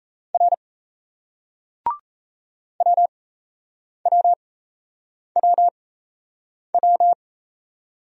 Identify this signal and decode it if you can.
Morse code, heard as a keyed tone, about 37 wpm, spaced as for 8 wpm, 700 Hz. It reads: R WWWW